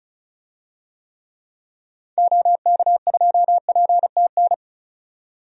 Morse code, heard as a keyed tone, about 36 wpm, 700 Hz. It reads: OK2PTN
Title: DXCC-100 Call Signs - 6 Characters - 35wpm